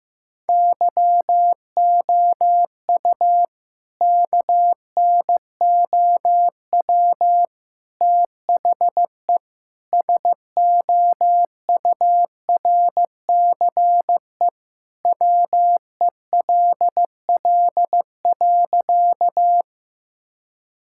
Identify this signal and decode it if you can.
Morse code, heard as a keyed tone, about 15 wpm, 700 Hz. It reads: YOU KNOW THE SOURCE WELL.
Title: Sentences from Top 1000 Words - Review Part 5 - 15wpm